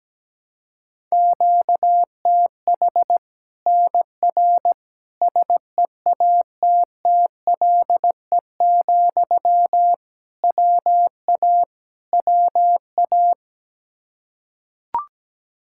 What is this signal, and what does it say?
Morse code, heard as a keyed tone, about 17 wpm, 700 Hz. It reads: QTH NR SEATTLE, WA WA
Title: Practice QSO - Exchange 1 - Beginning - 17wpm